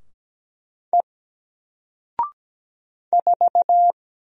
Morse code, heard as a keyed tone, about 17 wpm, 700 Hz. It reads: E 4
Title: Single Character (Letter:Number:Punctuation:Prosign) 17wpm